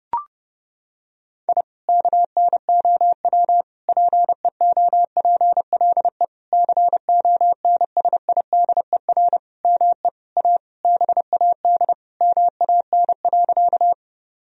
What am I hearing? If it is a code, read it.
Morse code, heard as a keyed tone, about 32 wpm, 700 Hz. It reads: I KNOW PEOPLE CONSIDER ME A BAD MAN.
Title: Sentences from Top 1000 Words - Review Part 4 - 30wpm